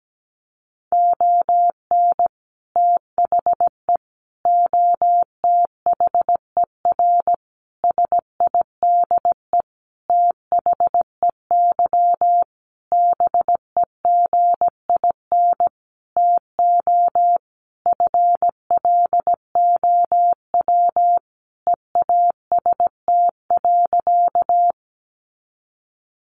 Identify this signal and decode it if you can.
Morse code, heard as a keyed tone, about 17 wpm, 700 Hz. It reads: ON THE OTHER SIDE THEY BEGIN TO FLOW EAST.
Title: Sentences from Top 1200 Words 17wpm